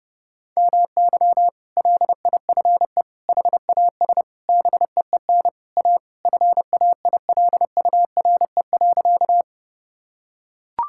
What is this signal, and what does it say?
Morse code, heard as a keyed tone, about 32 wpm, 700 Hz. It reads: MY LIFE HAS BEEN A FAILURE.
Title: Sentences from Top 1000 Words - Review Part 6 - 30wpm